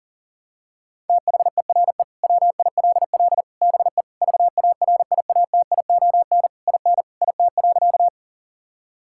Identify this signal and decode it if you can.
Morse code, heard as a keyed tone, about 40 wpm, 700 Hz. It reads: THERE WILL BE VARIATION IN IT.
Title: Sentences from Top 1600 Words 40wpm